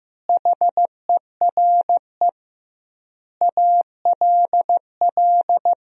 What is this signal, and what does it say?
Morse code, heard as a keyed tone, about 15 wpm, 700 Hz. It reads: HERE ALL
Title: 2x Word-Spacing - Sentences from Top 500 Words - 15wpm